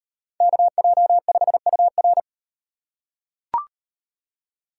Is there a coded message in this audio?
Yes: Morse code from 0.4 to 2.2 s, heard as a keyed tone, about 34 wpm, 700 Hz. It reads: KJ5UR